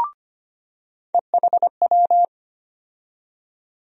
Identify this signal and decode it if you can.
Morse code, heard as a keyed tone, about 25 wpm, 700 Hz. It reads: EHW
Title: USA Call Sign Postfixes - 3 Letters 25wpm